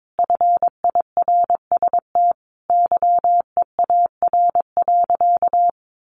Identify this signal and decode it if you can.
Morse code, heard as a keyed tone, about 22 wpm, 700 Hz. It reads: FIRST YEAR.
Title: Sentences from Top 1300 Words 22wpm